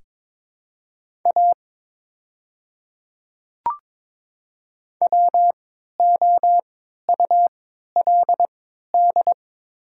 Morse code, heard as a keyed tone, about 22 wpm, spaced as for 12 wpm, 700 Hz. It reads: A WOULD
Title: 3x Character Spacing - Top 100 Words - 22wpm